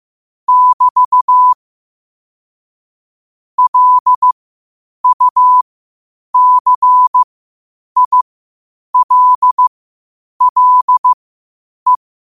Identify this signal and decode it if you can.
Morse code, heard as a keyed tone, about 15 wpm, spaced as for 8 wpm, 1000 Hz. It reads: <BT> LUCILLE
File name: qotdfiles / 8wpm